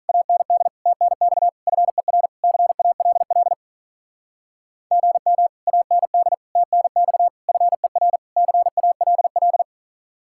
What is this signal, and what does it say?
Morse code, heard as a keyed tone, about 46 wpm, 700 Hz. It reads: AND TNX FER CALL GM AND TNX FER CALL